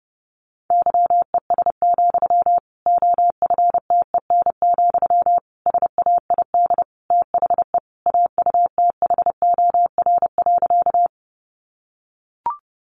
Morse code, heard as a keyed tone, about 30 wpm, 700 Hz. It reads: YES, OFTEN, SAID THE AUTHOR.
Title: Sentences from Top 1200 Words 30wpm